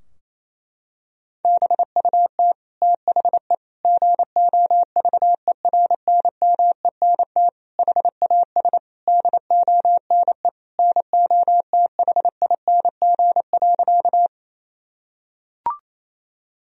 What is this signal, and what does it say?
Morse code, heard as a keyed tone, about 28 wpm, 700 Hz. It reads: BUT THE GOVERNMENT HAS DONE NOTHING.
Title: Sentences from Top 1000 Words - Review Part 3 - 28wpm